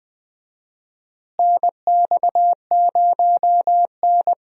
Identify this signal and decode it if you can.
Morse code, heard as a keyed tone, about 21 wpm, 700 Hz. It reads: NX0N
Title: USA Call Signs - 4 Characters - Repeated 2x - 20wpm